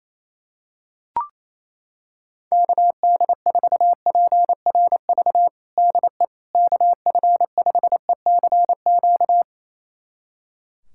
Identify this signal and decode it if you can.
Morse code, heard as a keyed tone, about 28 wpm, 700 Hz. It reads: KD4PRV DE KF5ECQ
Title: USA Callsign DE Callsign - 6 Characters - 28wpm